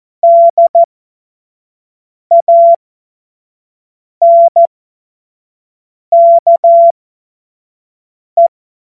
Morse code, heard as a keyed tone, about 14 wpm, spaced as for 5 wpm, 680 Hz. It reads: DANKE
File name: Morsekurs / Witze / Witz 19. Oktober 2021, 25 F70